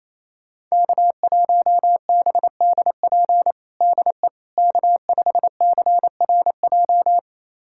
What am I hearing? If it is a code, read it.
Morse code, heard as a keyed tone, about 28 wpm, 700 Hz. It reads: K1BDP DE K5CRJ
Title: USA Callsign DE Callsign - 5 Characters - 28wpm